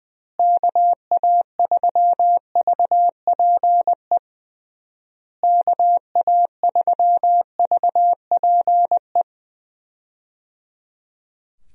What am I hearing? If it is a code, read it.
Morse code, heard as a keyed tone, about 20 wpm, 700 Hz. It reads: KA3VPE KA3VPE
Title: USA Call Signs - 6 Characters - Repeated 3x - 20wpm